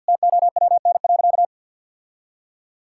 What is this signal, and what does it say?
Morse code, heard as a keyed tone, about 51 wpm, 700 Hz. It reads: TOWN.
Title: Sentences from Top 1000 Words - Review Part 5 - 50wpm